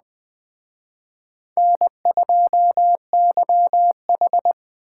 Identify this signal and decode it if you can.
Morse code, heard as a keyed tone, about 20 wpm, 700 Hz. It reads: N2YH
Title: Speed Racing - USA 1x2 Call Signs 17wpm